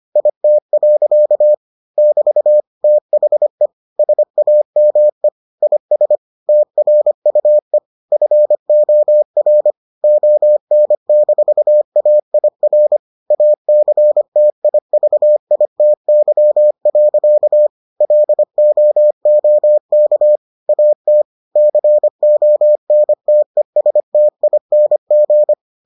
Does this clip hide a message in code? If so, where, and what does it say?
Yes: Morse code from 0.2 to 25.5 s, heard as a keyed tone, about 25 wpm, 600 Hz. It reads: IT. <BT> THE SAME IS TRUE FOR ON-AIR ACTIVITY. LOOK AT CONTESTING